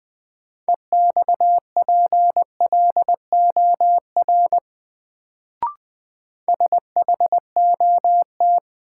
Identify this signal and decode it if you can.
Morse code, heard as a keyed tone, about 20 wpm, 700 Hz. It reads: EXPLOR SHOT
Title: Sentences from Top 1100 Words - New Word Drill Alt - 20wpm